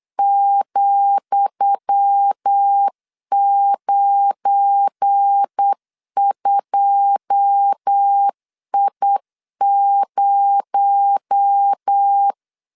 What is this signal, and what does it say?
Morse code, heard as a keyed tone, about 8 wpm, 800 Hz. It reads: ,92I0
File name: LEZIONI / 10 SIGNS / 10D COMA 8wpm 3749